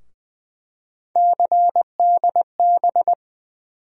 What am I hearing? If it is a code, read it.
Morse code, heard as a keyed tone, about 20 wpm, 700 Hz. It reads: CDB
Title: USA Call Sign Postfixes - 3 Letters 20wpm